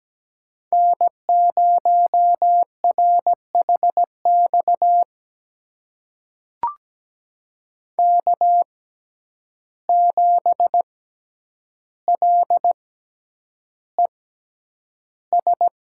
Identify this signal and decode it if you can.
Morse code, heard as a keyed tone, about 17 wpm, spaced as for 9 wpm, 700 Hz. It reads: N0RHX K7LES
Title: USA Call Signs - 5 Characters - 6x Character Spacing - 17wpm